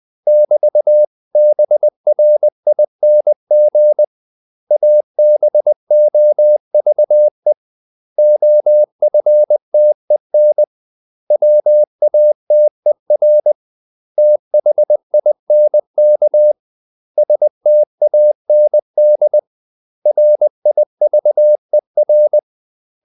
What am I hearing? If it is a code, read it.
Morse code, heard as a keyed tone, about 20 wpm, 600 Hz. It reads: <BT>BRING ABOVE OFTEN WATER THINK STAND RIVER
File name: CW / Practice / 500MostCommonEnglishWords20